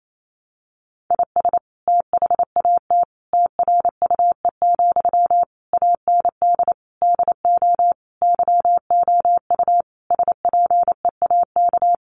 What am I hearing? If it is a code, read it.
Morse code, heard as a keyed tone, about 28 wpm, 700 Hz. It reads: IS THAT TRUE, AND DO YOU SPEAK
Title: Sentences from Top 1200 Words 28wpm